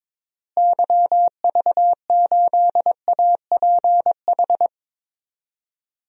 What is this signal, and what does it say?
Morse code, heard as a keyed tone, about 22 wpm, 700 Hz. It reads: YV8APH
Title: DXCC-100 Call Signs - 6 Characters - Repeated 3x - 22wpm